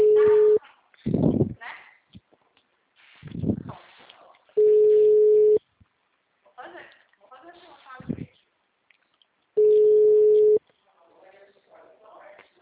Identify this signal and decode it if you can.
Ringback tone (Europe, 425 Hz), with speech.
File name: out- -2003- - - .52